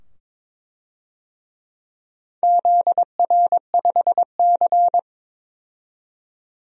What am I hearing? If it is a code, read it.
Morse code, heard as a keyed tone, about 22 wpm, 700 Hz. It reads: ZR5C
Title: DXCC-100 Call Signs - 4 Characters - 22wpm